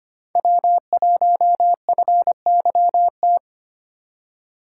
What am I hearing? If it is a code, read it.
Morse code, heard as a keyed tone, about 29 wpm, 700 Hz. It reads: W1FYT
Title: Speed Racing - USA 1x3 Call Signs 25wpm